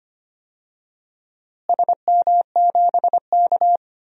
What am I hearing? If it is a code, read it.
Morse code, heard as a keyed tone, about 25 wpm, 700 Hz. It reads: SM7K
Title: DXCC-100 Call Signs - 4 Characters - Repeated 2x - 25wpm